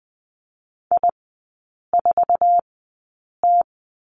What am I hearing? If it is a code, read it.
Morse code, heard as a keyed tone, about 20 wpm, spaced as for 8 wpm, 700 Hz. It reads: I4T
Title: 053 Lesson 10e - Letter r - Three Characters 20wpm